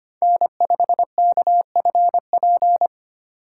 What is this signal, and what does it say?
Morse code, heard as a keyed tone, about 25 wpm, 700 Hz. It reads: N5KFP
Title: USA Call Signs - 5 Characters - 2x Character Spacing - 25wpm